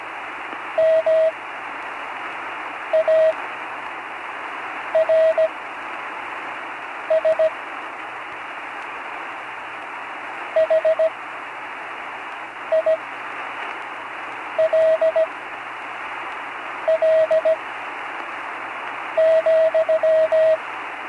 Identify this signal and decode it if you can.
Morse code, heard as a keyed tone, about 17 wpm, spaced as for 5 wpm, 640 Hz, with radio noise. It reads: MARS HILL,
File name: CW Learning Programs / Learning Morse Code (from Tune in the World) / 43 on the air